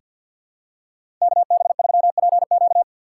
Morse code, heard as a keyed tone, about 50 wpm, 700 Hz. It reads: KB3PQ
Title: USA Call Signs - 5 Characters - 2x Character Spacing - 50wpm